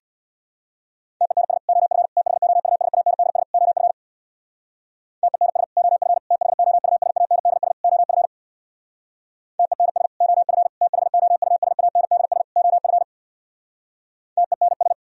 Morse code, heard as a keyed tone, about 62 wpm, 700 Hz. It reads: TENS OF THOUSANDS OF TENS OF THOUSANDS OF TENS OF THOUSANDS OF TENS
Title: Speed-Racing - Sets of 4 Words 50wpm